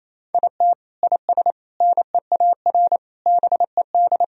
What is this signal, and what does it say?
Morse code, heard as a keyed tone, about 28 wpm, 700 Hz. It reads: IT IS NEAR BED
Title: Sentences from Top 1000 Words - Review Part 1 - 28wpm